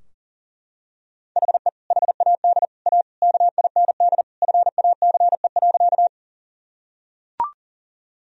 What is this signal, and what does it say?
Morse code, heard as a keyed tone, about 40 wpm, 700 Hz. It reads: HE HAD A KIND FACE.
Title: Sentences from Top 1000 Words - Review Part 2 - 40wpm